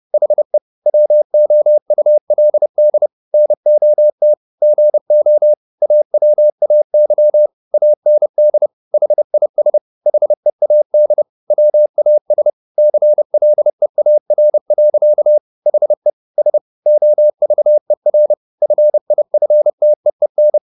Morse code, heard as a keyed tone, about 30 wpm, 600 Hz. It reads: HE WOULD NOT GO AWAY AND HIS HEAD WAS CLEAR. HE S OVER FIFTEEN